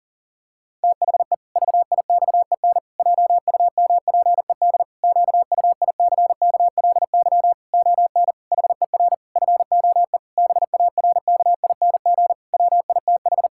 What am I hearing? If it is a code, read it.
Morse code, heard as a keyed tone, about 40 wpm, 700 Hz. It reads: THE VIXEN JUMPED QUICKLY ON HER FOE BARKING WITH
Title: Pangram Challenge v2 - 40wpm